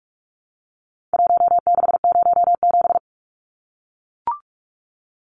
Morse code, heard as a keyed tone, about 44 wpm, 700 Hz. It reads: 1607
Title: Numbers - Contest Style - 4-Digit 45wpm